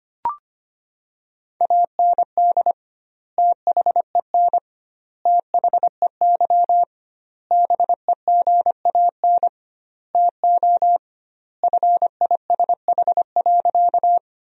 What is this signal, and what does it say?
Morse code, heard as a keyed tone, about 25 wpm, 700 Hz. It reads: AND THEN THEY BEGAN TO FISH.
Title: 2x Word-Spacing - Sentences from Top 300 Words - 25wpm